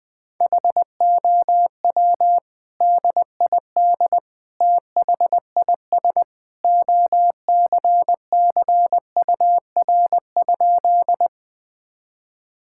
Morse code, heard as a keyed tone, about 20 wpm, 700 Hz. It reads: HOW DID THIS OCCUR?
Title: Sentences from Top 1000 Words - Review Part 6 - 20wpm